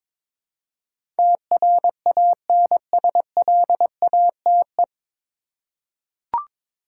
Morse code, heard as a keyed tone, about 22 wpm, 700 Hz. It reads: TRANSLATE
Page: Morse Code Ninja practice 22wpm